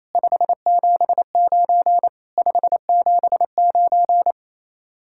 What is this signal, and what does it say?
Morse code, heard as a keyed tone, about 33 wpm, 700 Hz. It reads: 579 579
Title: Speed-Racing Common QSO Phrases - 28wpm